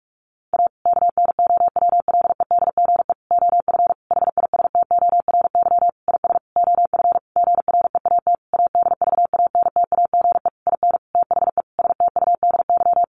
Morse code, heard as a keyed tone, about 44 wpm, 700 Hz. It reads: A KNOWLEDGE OF HISTORY IS OF GREAT ADVANTAGE IN THE STUDY
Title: Sentences from Top 900 Words 45wpm